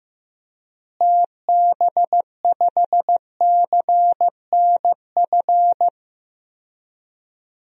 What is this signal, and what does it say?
Morse code, heard as a keyed tone, about 15 wpm, 700 Hz. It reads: TB5CNF
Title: DXCC-100 Call Signs - 6 Characters - Repeated 2x - 15wpm